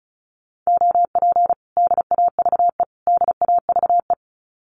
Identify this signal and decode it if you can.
Morse code, heard as a keyed tone, about 34 wpm, 700 Hz. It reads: OP DAVE DAVE
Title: Common QSO Phrases Encore 35wpm